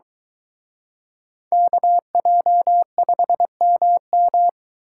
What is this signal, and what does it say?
Morse code, heard as a keyed tone, about 22 wpm, 700 Hz. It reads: KJ5MM